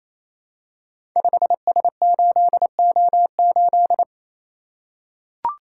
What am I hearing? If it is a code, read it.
Morse code, heard as a keyed tone, about 29 wpm, 700 Hz. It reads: 5S8O8
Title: Code Groups - 5 Letter:Numbers - 28wpm